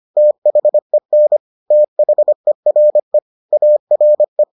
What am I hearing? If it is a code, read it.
Morse code, heard as a keyed tone, about 25 wpm, 600 Hz. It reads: THEN THERE ARE